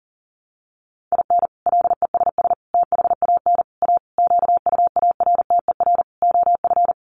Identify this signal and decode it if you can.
Morse code, heard as a keyed tone, about 40 wpm, 700 Hz. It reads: IN LESS THAN A QUARTER OF
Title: Sentences from Top 1300 Words 40wpm